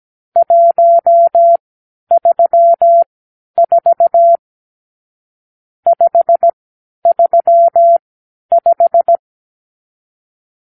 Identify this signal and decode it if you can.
Morse code, heard as a keyed tone, about 17 wpm, spaced as for 10 wpm, 680 Hz. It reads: 134 535
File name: arrl / Review (1-5)